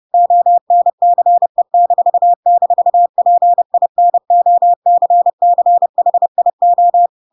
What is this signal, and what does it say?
Morse code, heard as a keyed tone, about 30 wpm, 700 Hz. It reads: ONCE--PINOCCHIO
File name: Pinocchio / Pinocchio ch3 30 30 ik1hge